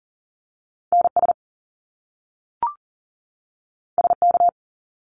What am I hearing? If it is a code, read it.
Morse code, heard as a keyed tone, about 40 wpm, 700 Hz. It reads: NS SK
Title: Canadian Provinces and Territories 40wpm